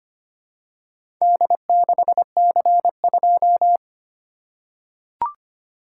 Morse code, heard as a keyed tone, about 25 wpm, 700 Hz. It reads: D6C2
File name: practice / 25wpm / Four Letter-Number Combinations - 25wpm